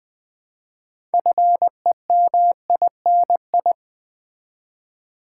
Morse code, heard as a keyed tone, about 20 wpm, 700 Hz. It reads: FEMINI
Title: Build A Word - 2 Letters At A Time - 8-Letter Words - 20wpm